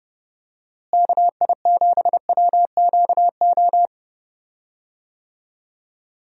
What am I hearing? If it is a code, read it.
Morse code, heard as a keyed tone, about 30 wpm, 700 Hz. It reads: KI7WQO
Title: USA Call Signs - 6 Characters - Repeated 3x - 30wpm